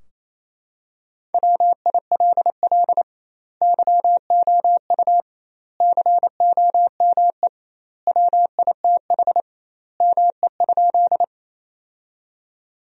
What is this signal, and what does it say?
Morse code, heard as a keyed tone, about 28 wpm, 700 Hz. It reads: WILL YOU COME WITH ME?
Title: 2x Word-Spacing - Sentences from Top 200 Words - 28wpm